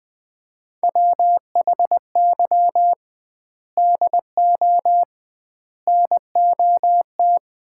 Morse code, heard as a keyed tone, about 20 wpm, 700 Hz. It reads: WHY DO NOT
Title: 2x Word-Spacing - Sets of 5 Words - 20wpm